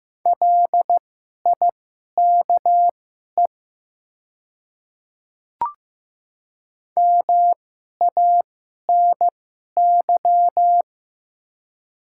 Morse code, heard as a keyed tone, about 15 wpm, spaced as for 11 wpm, 690 Hz. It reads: LIKE MANY
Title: 2x Character Spacing - Top 100 Words - 15wpm